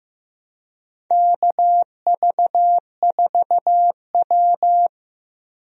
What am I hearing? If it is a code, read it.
Morse code, heard as a keyed tone, about 15 wpm, 700 Hz. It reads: KV4W
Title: USA Call Signs - 4 Characters - Repeated 3x - 15wpm